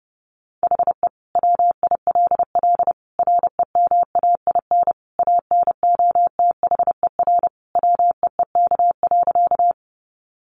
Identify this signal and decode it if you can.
Morse code, heard as a keyed tone, about 30 wpm, 700 Hz. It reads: HE WILL REMAIN ANOTHER WEEK.
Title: Sentences from Top 1200 Words 30wpm